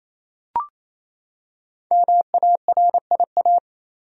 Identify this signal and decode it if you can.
Morse code, heard as a keyed tone, about 28 wpm, 700 Hz. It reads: MARIA
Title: USA Common Names - 28wpm